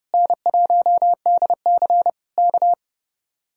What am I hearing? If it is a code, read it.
Morse code, heard as a keyed tone, about 29 wpm, 700 Hz. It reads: N1DC K
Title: Speed-Racing Common QSO Phrases - 30wpm